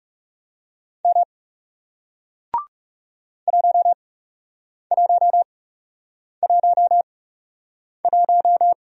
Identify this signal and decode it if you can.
Morse code, heard as a keyed tone, about 41 wpm, spaced as for 8 wpm, 700 Hz. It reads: M 1111